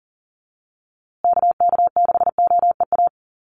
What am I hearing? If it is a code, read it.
Morse code, heard as a keyed tone, about 40 wpm, 700 Hz. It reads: KK6OEA
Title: USA Call Signs Repeated 2x 40wpm